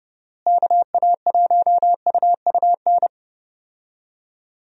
Morse code, heard as a keyed tone, about 30 wpm, 700 Hz. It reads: KA1UUN
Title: USA Call Signs - 6 Characters - Repeated 2x - 30wpm